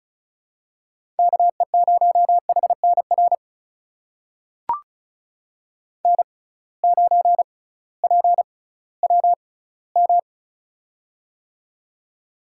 Morse code, heard as a keyed tone, about 34 wpm, spaced as for 19 wpm, 700 Hz. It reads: KE0HNR N9PWM
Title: USA Call Signs - 6x Character Spacing - 35wpm